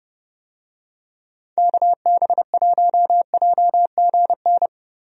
Morse code, heard as a keyed tone, about 32 wpm, 700 Hz. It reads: KB1JGN
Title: USA Call Signs - 6 Characters - Repeated 3x - 30wpm